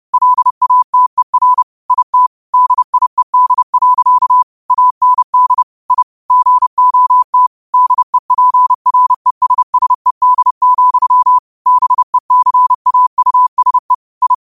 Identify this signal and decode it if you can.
Morse code, heard as a keyed tone, about 30 wpm, 1000 Hz. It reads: LATER IT DIED. AND I GOT DEPRESSED, BECAUSE I
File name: qotdfiles / 30wpm